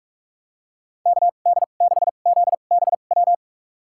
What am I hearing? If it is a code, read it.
Morse code, heard as a keyed tone, about 46 wpm, spaced as for 32 wpm, 700 Hz. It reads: KD6ZBW
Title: USA Call Signs - 6 Characters - 2x Character Spacing - 45wpm